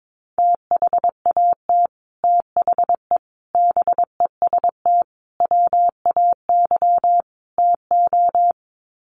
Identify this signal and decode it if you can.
Morse code, heard as a keyed tone, about 22 wpm, 700 Hz. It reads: THAT THE BEST WAY TO
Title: Sets of 5 Words Encore2 22wpm